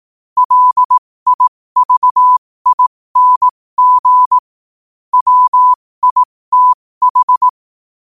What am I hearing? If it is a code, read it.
Morse code, heard as a keyed tone, about 18 wpm, spaced as for 15 wpm, 1000 Hz. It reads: LIVING WITH
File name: qotdfiles / 15wpm